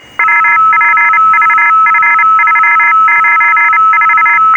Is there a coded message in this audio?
Yes: Morse code from 0.2 to 4.4 s, heard as a keyed tone, about 29 wpm, 1700 Hz, with noise. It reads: WPVF394